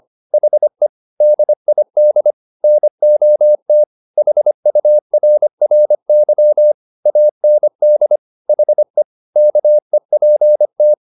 Morse code, heard as a keyed tone, about 25 wpm, 600 Hz. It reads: HE DID NOT HURRY AND HE KEPT
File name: the old man and the sea part 1 25wpm